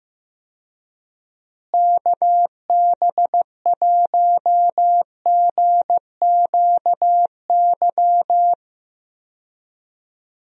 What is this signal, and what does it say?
Morse code, heard as a keyed tone, about 16 wpm, 700 Hz. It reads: KB1GQY
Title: USA Call Signs - 6 Characters - Repeated 3x - 15wpm